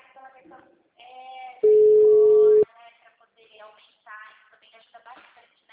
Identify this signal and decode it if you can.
Ringback tone (Europe, 425 Hz), with speech.
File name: out- -2002- - - .31122